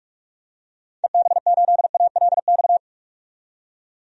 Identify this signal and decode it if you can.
Morse code, heard as a keyed tone, about 45 wpm, 700 Hz. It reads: EB8ALX
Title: DXCC-100 Call Signs - 6 Characters - 45wpm